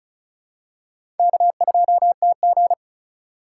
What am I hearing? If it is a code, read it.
Morse code, heard as a keyed tone, about 36 wpm, 700 Hz. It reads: K2TG
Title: USA Call Signs - 4 Characters - 3x Character Spacing - 35wpm